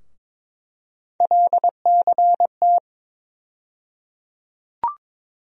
Morse code, heard as a keyed tone, about 22 wpm, 700 Hz. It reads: LCT
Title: USA Call Sign Postfixes - 3 Letters 22wpm